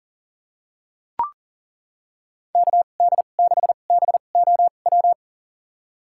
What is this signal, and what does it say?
Morse code, heard as a keyed tone, about 40 wpm, spaced as for 29 wpm, 700 Hz. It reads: KD6BOW
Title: USA Call Signs - 2x Character Spacing - 40wpm